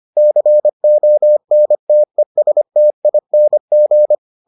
Morse code, heard as a keyed tone, about 25 wpm, 600 Hz. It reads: CONTESTING